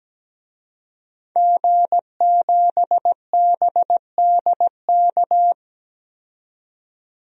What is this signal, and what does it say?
Morse code, heard as a keyed tone, about 17 wpm, 700 Hz. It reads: G7BDK
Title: DXCC-100 Call Signs - 5 Characters - 17wpm